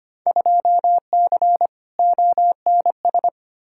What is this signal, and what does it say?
Morse code, heard as a keyed tone, about 25 wpm, 700 Hz. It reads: 2C ONS
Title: Field Day Exchanges Encore - 25wpm